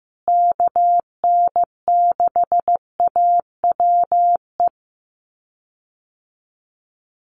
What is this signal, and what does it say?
Morse code, heard as a keyed tone, about 15 wpm, 700 Hz. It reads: KN6AWE
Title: USA Call Signs Repeated 3x 15wpm